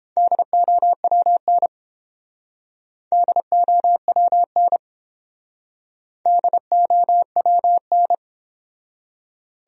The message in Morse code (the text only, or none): DOWN DOWN DOWN